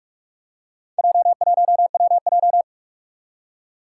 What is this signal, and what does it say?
Morse code, heard as a keyed tone, about 44 wpm, 700 Hz. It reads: J1WJ
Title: Four Challenging Letter-Number Combinations 45wpm